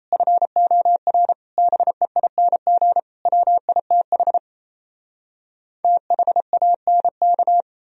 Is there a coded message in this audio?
Yes: Morse code from 0.1 to 7.6 s, heard as a keyed tone, about 38 wpm, 700 Hz. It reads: FOR BEING WITH THANK